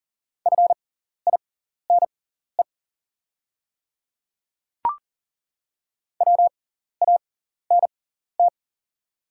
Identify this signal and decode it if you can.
Morse code, heard as a keyed tone, about 40 wpm, spaced as for 14 wpm, 690 Hz. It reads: FINE WANT
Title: 6x Character Spacing - Top 500 Words - 40wpm